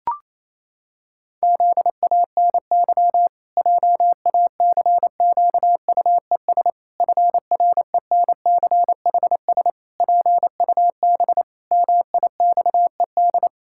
Morse code, heard as a keyed tone, about 28 wpm, 700 Hz. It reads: ZANY JACQUES FRENCHS PUB MIXED
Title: Pangram Challenge v2 - 28wpm